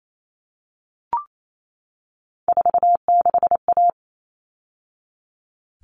Morse code, heard as a keyed tone, about 28 wpm, 700 Hz. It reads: 46A
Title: All Cut Numbers - 3-Digit 28wpm